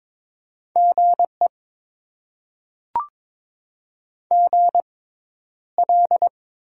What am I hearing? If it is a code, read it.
Morse code, heard as a keyed tone, about 22 wpm, spaced as for 11 wpm, 700 Hz. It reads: GE GL
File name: Top 90 QSO Elements - 6x Character Spacing - 22wpm